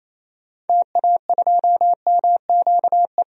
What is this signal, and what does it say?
Morse code, heard as a keyed tone, about 28 wpm, 700 Hz. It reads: TA2MQE
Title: DXCC-100 Call Signs - 6 Characters - 28wpm